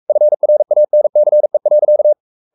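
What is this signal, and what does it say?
Morse code, heard as a keyed tone, about 44 wpm, 600 Hz. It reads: FRANCE.
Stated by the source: Morse code at 43 wpm sending Ken Follett's novel Never in English, from CW med Gnister.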